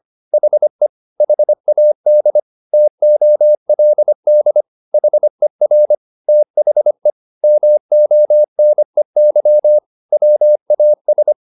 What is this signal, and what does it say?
Morse code, heard as a keyed tone, about 25 wpm, 600 Hz. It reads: HE HAD TOLD HER THE MONEY WAS